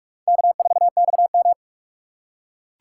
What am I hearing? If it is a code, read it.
Morse code, heard as a keyed tone, about 46 wpm, 700 Hz. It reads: K4XM
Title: USA Call Signs - 4 Characters - 3x Character Spacing - 45wpm